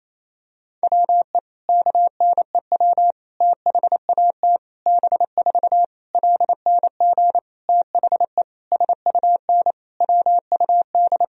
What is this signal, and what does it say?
Morse code, heard as a keyed tone, about 27 wpm, 700 Hz. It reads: WE KNEW THAT B4 LNG THE SUN WUD
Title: Sentences from Top 400 Words and Abbreviations 28wpm